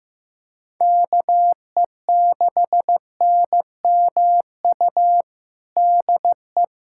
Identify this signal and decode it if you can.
Morse code, heard as a keyed tone, about 15 wpm, 700 Hz. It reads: KE6NMU DE
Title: USA Callsign DE Callsign 15wpm